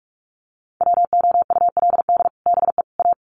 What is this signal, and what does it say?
Morse code, heard as a keyed tone, about 44 wpm, 700 Hz. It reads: WOULD BE A
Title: Sets of 3 Words 45wpm